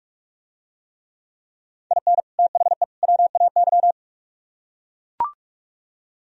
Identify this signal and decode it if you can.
Morse code, heard as a keyed tone, about 44 wpm, 700 Hz. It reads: IN THE WAY